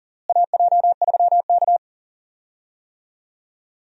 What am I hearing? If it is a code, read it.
Morse code, heard as a keyed tone, about 40 wpm, 700 Hz. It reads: AJ3K
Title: USA Call Signs - 4 Characters - Repeated 3x - 40wpm